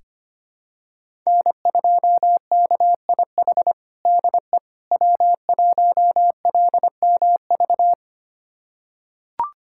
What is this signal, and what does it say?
Morse code, heard as a keyed tone, about 25 wpm, 700 Hz. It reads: N2KIH DE W1LMV
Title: USA Callsign DE Callsign - 5 Characters - 25wpm